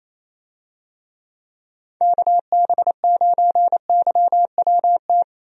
Morse code, heard as a keyed tone, about 28 wpm, 700 Hz. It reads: KB9YWT
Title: USA Call Signs - 6 Characters - 28wpm